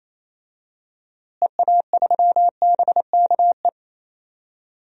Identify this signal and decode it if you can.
Morse code, heard as a keyed tone, about 27 wpm, 700 Hz. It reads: EA3BKE
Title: DXCC-100 Call Signs - 6 Characters - Repeated 3x - 28wpm